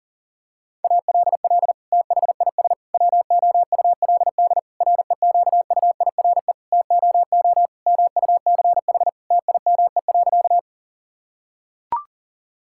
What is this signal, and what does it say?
Morse code, heard as a keyed tone, about 40 wpm, 700 Hz. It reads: ALL THIS WOULD REQUIRE TOO MUCH TIME.
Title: Sentences from Top 1000 Words - Review Part 6 - 40wpm